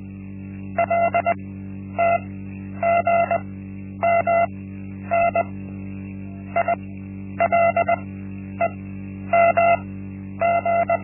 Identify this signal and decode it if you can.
Morse code, heard as a keyed tone, about 20 wpm, spaced as for 10 wpm, 680 Hz. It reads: LTGMN ILEMG
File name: Linksmakalnis-1